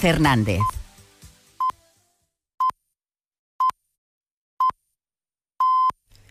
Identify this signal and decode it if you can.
Radio time signal (time pips), with speech. It is heard from 0.6 to 5.9 s.